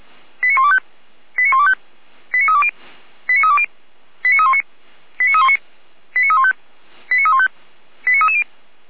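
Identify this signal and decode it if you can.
Five-tone selective call (ZVEI), heard as quick runs of five tones, tones of about 70 ms. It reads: 89125 89125 89320 89320 89319 89319 89315 89315 89330
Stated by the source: ZVEI five-tone selective call received by SDR.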